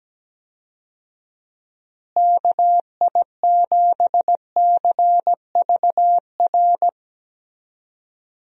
Morse code, heard as a keyed tone, about 17 wpm, 700 Hz. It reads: KI7CVR
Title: USA Call Signs - 6 Characters - Repeated 2x - 17wpm